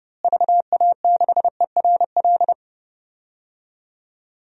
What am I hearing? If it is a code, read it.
Morse code, heard as a keyed tone, about 30 wpm, 700 Hz. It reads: VA6ERL
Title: DXCC-100 Call Signs - 6 Characters - Repeated 2x - 30wpm